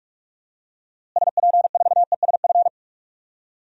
Morse code, heard as a keyed tone, about 42 wpm, 700 Hz. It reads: SP4ESF
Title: DXCC-100 Call Signs - 6 Characters - Repeated 3x - 45wpm